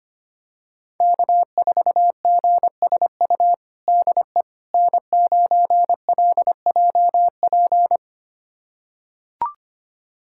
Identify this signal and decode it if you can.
Morse code, heard as a keyed tone, about 25 wpm, 700 Hz. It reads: K4GSU DE N9LJP
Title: USA Callsign DE Callsign 25wpm